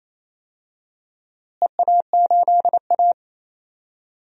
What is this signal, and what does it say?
Morse code, heard as a keyed tone, about 28 wpm, 700 Hz. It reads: EA8A
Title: DXCC-100 Call Signs - 4 Characters - Repeated 3x - 28wpm